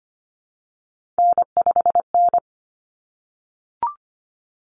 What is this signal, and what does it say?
Morse code, heard as a keyed tone, about 25 wpm, 700 Hz. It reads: N5N
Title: Common Cut Numbers - 3-Digit 25wpm